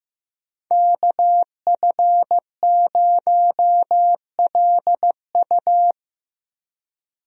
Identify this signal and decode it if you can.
Morse code, heard as a keyed tone, about 15 wpm, 700 Hz. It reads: KF0LU
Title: USA Call Signs - 5 Characters - Repeated 3x - 15wpm